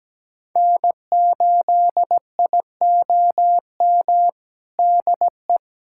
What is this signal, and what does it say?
Morse code, heard as a keyed tone, about 17 wpm, 700 Hz. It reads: N8IOM DE
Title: USA Callsign DE Callsign - 5 Characters - 17wpm